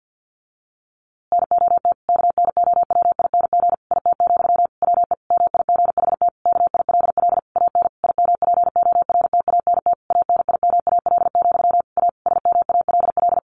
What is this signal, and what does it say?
Morse code, heard as a keyed tone, about 50 wpm, 700 Hz. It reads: NOT KNOWING IT, WE MIGHT KILL AN IMPORTANT ANIMAL, A SMALL
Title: A Sound Of Thunder 50wpm